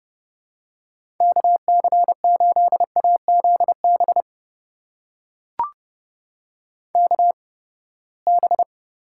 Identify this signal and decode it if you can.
Morse code, heard as a keyed tone, about 30 wpm, spaced as for 21 wpm, 700 Hz. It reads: KC8AZB KB
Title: USA Call Signs - 6 Characters - 8x Character Spacing - 30wpm